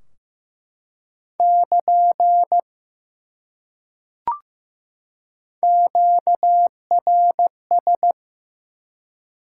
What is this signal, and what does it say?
Morse code, heard as a keyed tone, about 15 wpm, 700 Hz. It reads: <KN> QRS
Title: Top 90 QSO Elements 15wpm